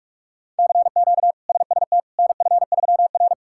Morse code, heard as a keyed tone, about 46 wpm, 700 Hz. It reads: CQ SST NF3R
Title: K1USN SST QSO Practice - 45wpm